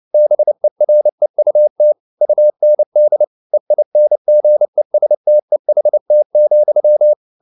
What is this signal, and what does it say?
Morse code, heard as a keyed tone, about 29 wpm, 600 Hz. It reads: BEREUT UND EINGESTEHT,